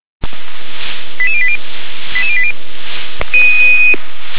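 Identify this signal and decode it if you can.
Five-tone selective call (ZVEI), heard as quick runs of five tones, tones of about 80 ms. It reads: 88080 88080
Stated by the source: ZVEI five-tone selective call 88702 received by SDR.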